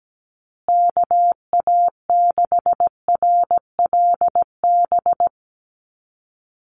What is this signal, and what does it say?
Morse code, heard as a keyed tone, about 17 wpm, 700 Hz. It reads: KA6RLB
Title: USA Call Signs Repeated 3x 17wpm